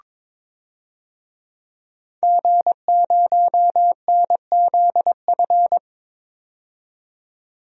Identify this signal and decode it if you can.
Morse code, heard as a keyed tone, about 22 wpm, 700 Hz. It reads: G0NZF
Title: DXCC-100 Call Signs - 5 Characters - 22wpm